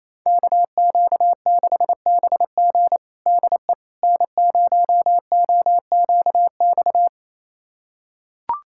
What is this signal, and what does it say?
Morse code, heard as a keyed tone, about 27 wpm, 700 Hz. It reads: KQ6BG DE N0OQX
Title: USA Callsign DE Callsign - 5 Characters - 28wpm